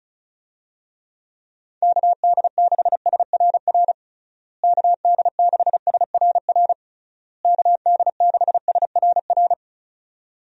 Morse code, heard as a keyed tone, about 34 wpm, 700 Hz. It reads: KD6SRR KD6SRR KD6SRR
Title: USA Call Signs - 6 Characters - Repeated 3x - 35wpm